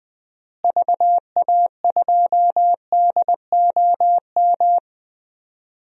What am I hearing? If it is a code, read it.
Morse code, heard as a keyed tone, about 20 wpm, 700 Hz. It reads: VA2DOM
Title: DXCC-100 Call Signs - 6 Characters - Repeated 3x - 20wpm